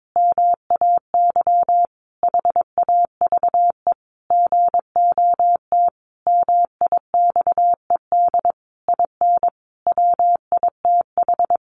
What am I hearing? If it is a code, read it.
Morse code, heard as a keyed tone, about 22 wpm, 700 Hz. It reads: MAY HAVE GOT MIXED IN WITH
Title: Sentences from Top 1500 Words 22wpm